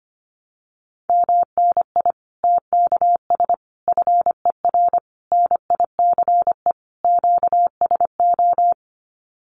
Morse code, heard as a keyed tone, about 25 wpm, 700 Hz. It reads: MNI TKS FER NICE QSO
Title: Common QSO Phrases 25wpm